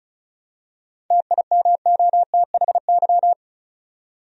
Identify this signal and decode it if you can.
Morse code, heard as a keyed tone, about 36 wpm, 700 Hz. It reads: TIMOTHY
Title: USA Common Names - 35wpm